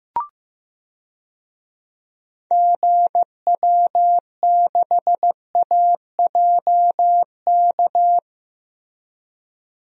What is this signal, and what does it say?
Morse code, heard as a keyed tone, about 15 wpm, 700 Hz. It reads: GW6AJK
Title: DXCC-100 Call Signs - 6 Characters - Repeated 2x - 15wpm